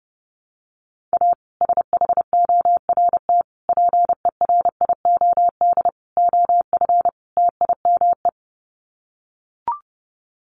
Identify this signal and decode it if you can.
Morse code, heard as a keyed tone, about 30 wpm, 700 Hz. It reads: A SHORT PERIOD OF TIME
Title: Sets of 5 Words Encore 30wpm